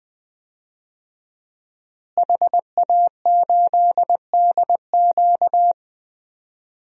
Morse code, heard as a keyed tone, about 20 wpm, 700 Hz. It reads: HA8DQ
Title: DXCC-100 Call Signs - 5 Characters - Repeated 3x - 20wpm